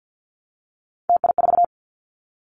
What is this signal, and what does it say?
Morse code, heard as a keyed tone, about 48 wpm, 700 Hz. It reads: TI4
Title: Callsign Prefixes for Top 100 DXCC - 50wpm